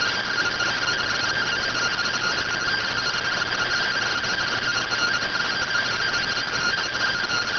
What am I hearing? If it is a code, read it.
Two-tone teleprinter signal, about 170 Hz shift.